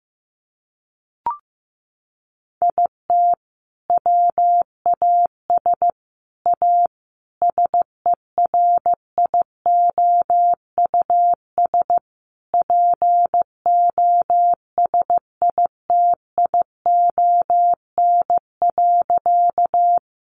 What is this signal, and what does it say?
Morse code, heard as a keyed tone, about 15 wpm, 700 Hz. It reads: IT WAS A SERIOUS POSITION.
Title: Sentences from Top 1100 Words 15wpm